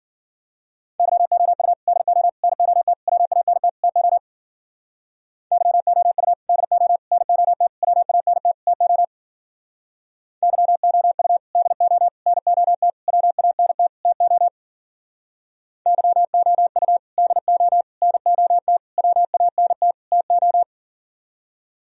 Morse code, heard as a keyed tone, about 52 wpm, 700 Hz. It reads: YOU DO NOT WANT TO YOU DO NOT WANT TO YOU DO NOT WANT TO YOU DO NOT WANT TO